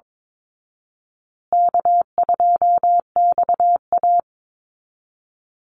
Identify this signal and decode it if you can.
Morse code, heard as a keyed tone, about 22 wpm, 700 Hz. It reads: K2XA
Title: Easy Call Signs Repeated 3x - 22wpm